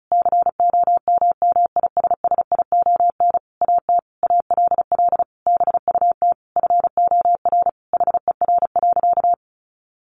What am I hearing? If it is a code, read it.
Morse code, heard as a keyed tone, about 34 wpm, 700 Hz. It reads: COMMISSION AT ALL BUT FOR HER.
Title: Sentences from Top 1400 Words 35wpm